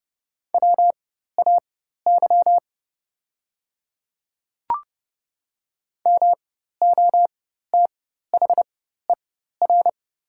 Morse code, heard as a keyed tone, about 30 wpm, spaced as for 14 wpm, 700 Hz. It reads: WAY MOTHER
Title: 4x Character Spacing - Top 200 Words - 30wpm